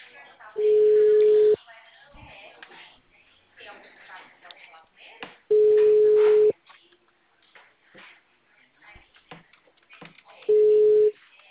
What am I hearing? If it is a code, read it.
Ringback tone (Europe, 425 Hz), with speech.
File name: out- -2007- - - .8996